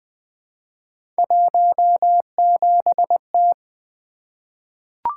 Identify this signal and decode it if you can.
Morse code, heard as a keyed tone, about 20 wpm, 700 Hz. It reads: 17T
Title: Three Letter-Number Combinations - 20wpm